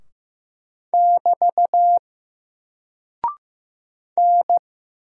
Morse code, heard as a keyed tone, about 15 wpm, 700 Hz. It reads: <BT> N
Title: Single Character (Letter:Number:Punctuation:Prosign) 15wpm